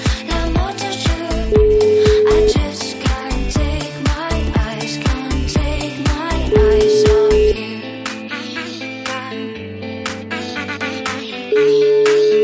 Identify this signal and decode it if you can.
Ringback tone (Europe, 425 Hz), with noise.